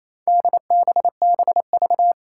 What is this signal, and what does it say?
Morse code, heard as a keyed tone, about 28 wpm, 700 Hz. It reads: DBBV
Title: Four Challenging Letter-Number Combinations 28wpm